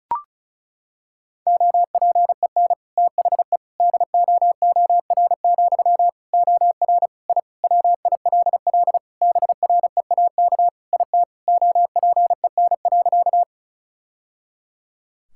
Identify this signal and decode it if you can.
Morse code, heard as a keyed tone, about 36 wpm, 700 Hz. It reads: OPEN THE DOOR, OR I WILL BREAK IT OPEN.
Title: Sentences from Top 1000 Words - Review Part 6 - 35wpm